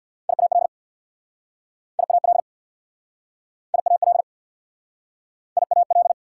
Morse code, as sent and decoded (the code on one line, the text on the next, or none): .. .- .-.. / .. .- .-.. / .. .- .-.. / .. .- .-..
IAL IAL IAL IAL